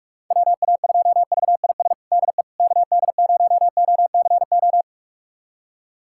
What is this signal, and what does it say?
Morse code, heard as a keyed tone, about 46 wpm, 700 Hz. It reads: WA2VIS DE KD0OCO
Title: USA Callsign DE Callsign - 6 Characters - 45wpm